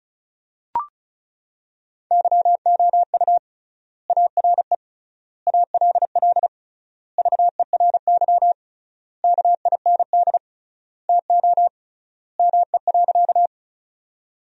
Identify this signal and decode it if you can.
Morse code, heard as a keyed tone, about 36 wpm, 700 Hz. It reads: YOU ARE ALL VERY KIND TO ME.
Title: 3x Word-Spacing - Sentences from Top 200 Words - 35wpm